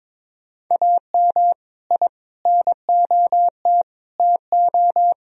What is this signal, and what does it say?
Morse code, heard as a keyed tone, about 21 wpm, 700 Hz. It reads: AM I NOT TO
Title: Sentences from Top 1000 Words - Review Part 2 - 22wpm